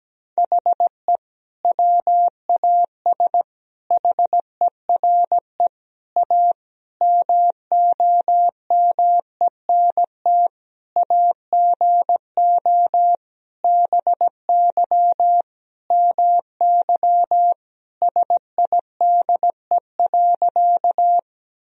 Morse code, heard as a keyed tone, about 17 wpm, 700 Hz. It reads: HE WAS HERE A MOMENT AGO BY MY SIDE.
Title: Sentences from Top 1000 Words - Review Part 3 - 17wpm